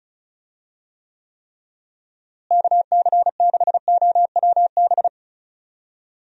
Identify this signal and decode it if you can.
Morse code, heard as a keyed tone, about 36 wpm, 700 Hz. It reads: KC6OWB
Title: USA Call Signs - 6 Characters - Repeated 3x - 35wpm